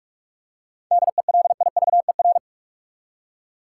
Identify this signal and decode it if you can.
Morse code, heard as a keyed tone, about 46 wpm, 700 Hz. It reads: DELIVER
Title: Seven Letter Words - 45wpm